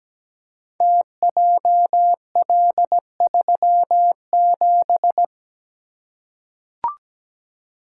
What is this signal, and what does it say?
Morse code, heard as a keyed tone, about 17 wpm, 700 Hz. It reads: TJL37
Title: Code Groups - 5 Letter:Numbers - 17wpm